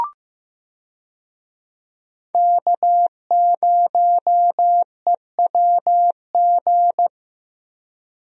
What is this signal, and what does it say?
Morse code, heard as a keyed tone, about 15 wpm, 700 Hz. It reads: K0EWG
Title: USA Call Signs - 5 Characters - Repeated 3x - 15wpm